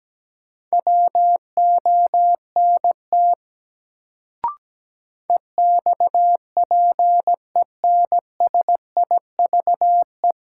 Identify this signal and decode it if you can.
Morse code, heard as a keyed tone, about 17 wpm, 700 Hz. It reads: WONT EXPENSIVE
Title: Sentences from Top 1000 Words - Review Part 6 - New Word Drill - 17wpm